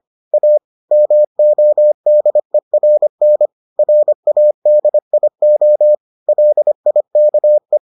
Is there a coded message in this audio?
Yes: Morse code from 0.3 to 7.8 s, heard as a keyed tone, about 25 wpm, 600 Hz. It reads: A MODERN RADIO LIKE